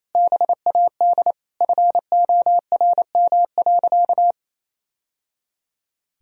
Morse code, heard as a keyed tone, about 28 wpm, 700 Hz. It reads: BAD FORM.
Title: Sentences from Top 1000 Words - Review Part 1 - 28wpm